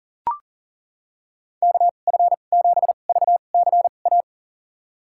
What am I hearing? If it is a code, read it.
Morse code, heard as a keyed tone, about 40 wpm, spaced as for 29 wpm, 700 Hz. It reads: KF7VCA
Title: USA Call Signs - 6 Characters - 2x Character Spacing - 40wpm